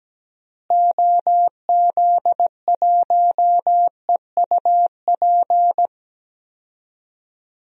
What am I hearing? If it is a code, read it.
Morse code, heard as a keyed tone, about 17 wpm, 700 Hz. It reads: OZ1EUP
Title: DXCC-100 Call Signs - 6 Characters - Repeated 2x - 17wpm